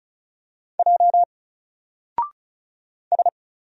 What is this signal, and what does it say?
Morse code, heard as a keyed tone, about 36 wpm, 700 Hz. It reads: J S
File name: Single-Letter-Number-35wpm